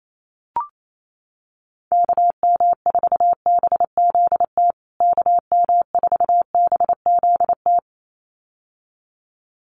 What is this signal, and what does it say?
Morse code, heard as a keyed tone, about 28 wpm, 700 Hz. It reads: KM4BZT KM4BZT
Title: USA Call Signs Repeated 2x 28wpm